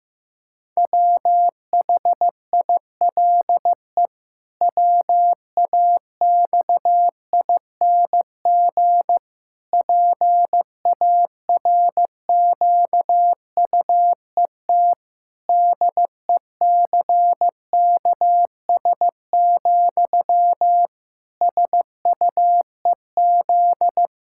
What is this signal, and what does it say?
Morse code, heard as a keyed tone, about 15 wpm, 700 Hz. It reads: WHILE WAXING PARQUET DECKS, SUEZ
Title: Pangram Challenge v2 - 15wpm